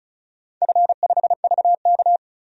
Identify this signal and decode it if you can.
Morse code, heard as a keyed tone, about 36 wpm, 700 Hz. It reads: F5VK
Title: DXCC-100 Call Signs - 4 Characters - 35wpm